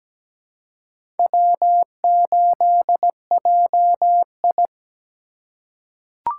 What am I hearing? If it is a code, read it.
Morse code, heard as a keyed tone, about 17 wpm, 700 Hz. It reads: W8JI
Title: USA Call Signs - 2x Character Spacing - 17wpm